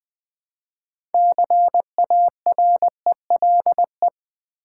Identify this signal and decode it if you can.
Morse code, heard as a keyed tone, about 20 wpm, 700 Hz. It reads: CARELE
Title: Build A Word - 3 Letters At A Time - 8-Letter Words - 20wpm